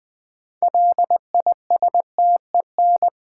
Morse code, heard as a keyed tone, about 20 wpm, 700 Hz. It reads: LISTEN
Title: 6 Letter Words - First 3 Letters Given - 20wpm